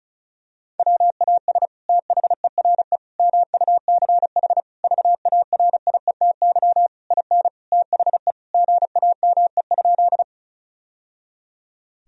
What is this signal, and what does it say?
Morse code, heard as a keyed tone, about 34 wpm, 700 Hz. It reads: WAS THERE MUCH VARIETY IN THE GAME?
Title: Sentences from Top 1000 Words - Review Part 4 - 35wpm